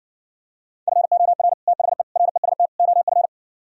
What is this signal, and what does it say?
Morse code, heard as a keyed tone, about 70 wpm, 700 Hz. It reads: FOR THE REST OF